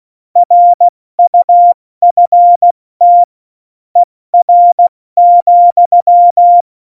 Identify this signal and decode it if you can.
Morse code, heard as a keyed tone, about 16 wpm, 700 Hz. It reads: RUFT ER,
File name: Morsekurs / Witze / Witz 17.Dezember 2019, 70 F80